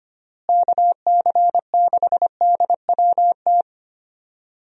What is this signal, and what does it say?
Morse code, heard as a keyed tone, about 25 wpm, 700 Hz. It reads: KC6DWT
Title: USA Call Signs - 6 Characters - Repeated 2x - 25wpm